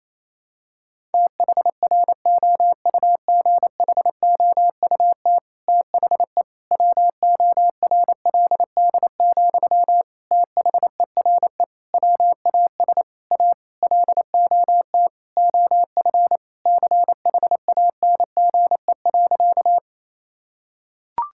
Morse code, heard as a keyed tone, about 28 wpm, 700 Hz. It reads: THROUGHOUT THE WORLD, THERE WAS A LOT OF CHANGE.
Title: Sentences from Top 1000 Words - Review Part 5 - 28wpm